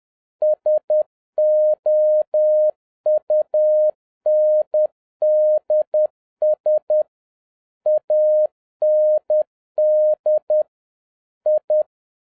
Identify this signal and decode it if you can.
Morse code, heard as a keyed tone, about 10 wpm, 610 Hz. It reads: SOUNDS AND I